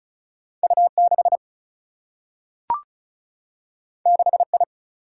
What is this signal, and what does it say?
Morse code, heard as a keyed tone, about 36 wpm, 700 Hz. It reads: U 6 6 I
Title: Single-Letter Single-Number 35wpm